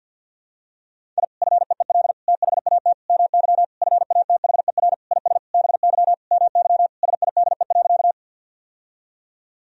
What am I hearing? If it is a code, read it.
Morse code, heard as a keyed tone, about 53 wpm, 700 Hz. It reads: I FEEL THAT MY FATHER IS BY MY SIDE.
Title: Sentences from Top 1000 Words - Review Part 1 - 50wpm